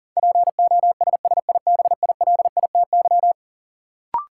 Morse code, heard as a keyed tone, about 40 wpm, 700 Hz. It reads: POSSIBILITY
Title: Sets of 3 Words - New Word Drill - 40wpm